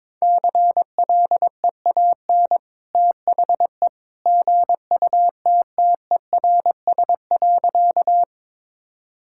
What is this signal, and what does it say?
Morse code, heard as a keyed tone, about 22 wpm, 700 Hz. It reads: CLEAN THE GUTTERS.
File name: practice / 22wpm / Easy Sentences with Hints - 22wpm